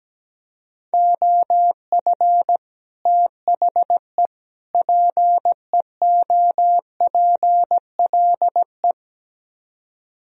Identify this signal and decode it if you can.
Morse code, heard as a keyed tone, about 17 wpm, 700 Hz. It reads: OF THE PEOPLE
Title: Speed-Racing - Sets of 3 Words 17wpm